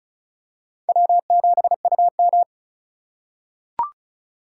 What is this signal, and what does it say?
Morse code, heard as a keyed tone, about 36 wpm, 700 Hz. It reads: W7UM
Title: USA Call Signs - 4x Character Spacing - 35wpm